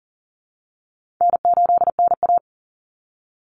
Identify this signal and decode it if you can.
Morse code, heard as a keyed tone, about 40 wpm, 700 Hz. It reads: N8NA
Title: Easy Call Signs - 40wpm